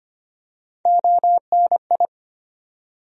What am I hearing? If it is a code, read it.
Morse code, heard as a keyed tone, about 25 wpm, 700 Hz. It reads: ONI
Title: USA Call Sign Postfixes - 3 Letters 25wpm